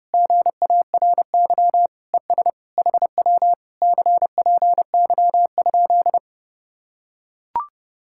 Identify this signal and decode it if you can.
Morse code, heard as a keyed tone, about 30 wpm, 700 Hz. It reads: GARY ES HW CPY?